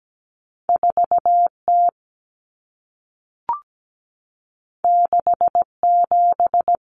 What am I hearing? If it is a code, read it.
Morse code, heard as a keyed tone, about 17 wpm, 700 Hz. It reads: 4T 67
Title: Common Cut Numbers - 2-Digit 17wpm